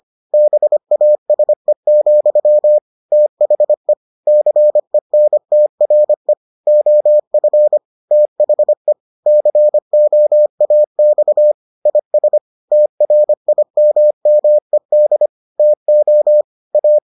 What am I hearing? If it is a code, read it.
Morse code, heard as a keyed tone, about 25 wpm, 600 Hz. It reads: BASE, THE CENTRE OF THE COAX IS TRIMMED TO A